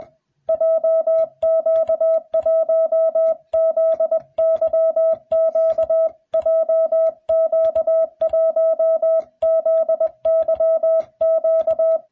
Morse code, heard as a keyed tone, about 21 wpm, 660 Hz. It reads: JQ1ZYQ JQ1ZYQ